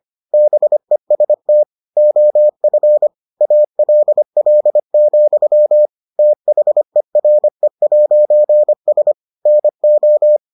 Morse code, heard as a keyed tone, about 25 wpm, 600 Hz. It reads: BEST OF ALL, THERE'S NO